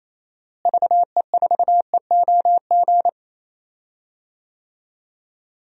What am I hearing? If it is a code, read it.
Morse code, heard as a keyed tone, about 28 wpm, 700 Hz. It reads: VE4EOG
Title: DXCC-100 Call Signs - 6 Characters - Repeated 3x - 28wpm